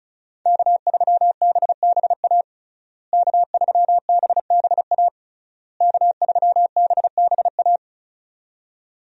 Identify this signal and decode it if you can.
Morse code, heard as a keyed tone, about 34 wpm, 700 Hz. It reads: K3BBA K3BBA K3BBA
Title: USA Call Signs - 5 Characters - Repeated 3x - 35wpm